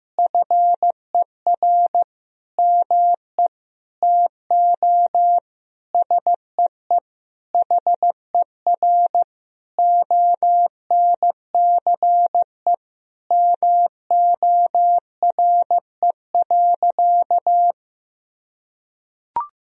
Morse code, heard as a keyed tone, about 16 wpm, 700 Hz. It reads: FER ME TO SEE HER ONCE MORE.
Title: Sentences from Top 500 Words and Abbreviations 15wpm